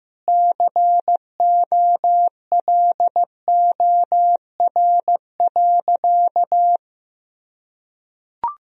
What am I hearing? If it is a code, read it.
Morse code, heard as a keyed tone, about 16 wpm, 700 Hz. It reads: COLOR.
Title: Sentences from Top 1000 Words - Review Part 2 - 15wpm